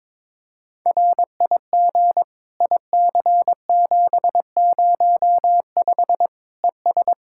Morse code, heard as a keyed tone, about 22 wpm, 700 Hz. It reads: RIG IC705 ES
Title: Practice QSO - Exchange 2 - Info - 22wpm